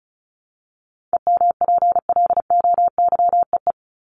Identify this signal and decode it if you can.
Morse code, heard as a keyed tone, about 34 wpm, 700 Hz. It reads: EMPLOYEE
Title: Top 1501-1750 Words 35wpm